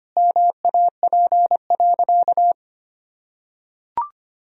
Morse code, heard as a keyed tone, about 25 wpm, 700 Hz. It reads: MAP.
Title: Sentences from Top 1000 Words - Review Part 2 - 25wpm